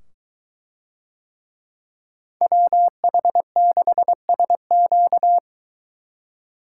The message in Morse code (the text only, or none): WH6SQ